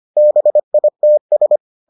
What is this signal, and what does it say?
Morse code, heard as a keyed tone, about 25 wpm, 600 Hz. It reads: BITS